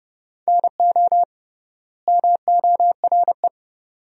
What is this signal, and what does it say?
Morse code, heard as a keyed tone, about 32 wpm, 700 Hz. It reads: NO MORE
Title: 3x Word-Spacing - Sentences from Top 400 Words - 30wpm